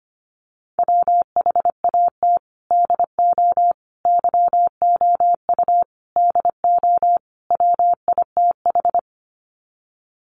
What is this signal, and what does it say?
Morse code, heard as a keyed tone, about 25 wpm, 700 Hz. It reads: WHAT DO YOU DO WITH
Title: Sets of 5 Words Encore2 25wpm